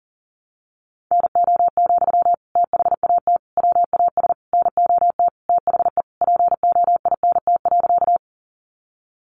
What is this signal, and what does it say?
Morse code, heard as a keyed tone, about 40 wpm, 700 Hz. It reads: NO, THAT WAS NOT THE POINT.
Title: Sentences from Top 600 Words 40wpm